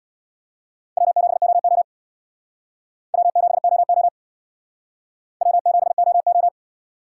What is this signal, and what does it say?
Morse code, heard as a keyed tone, about 69 wpm, 700 Hz. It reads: W7OO W7OO W7OO